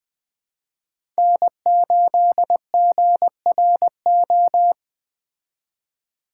Morse code, heard as a keyed tone, about 20 wpm, 700 Hz. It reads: N8GRO
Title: Speed Racing - USA 1x3 Call Signs 17wpm